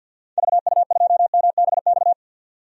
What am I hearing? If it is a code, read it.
Morse code, heard as a keyed tone, about 48 wpm, 700 Hz. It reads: VU2MBX
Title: DXCC-100 Call Signs - 6 Characters - 50wpm